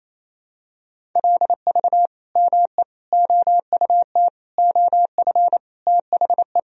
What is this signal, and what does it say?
Morse code, heard as a keyed tone, about 28 wpm, 700 Hz. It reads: LV ME OUT OF THE
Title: Sentences from Top 400 Words and Abbreviations 28wpm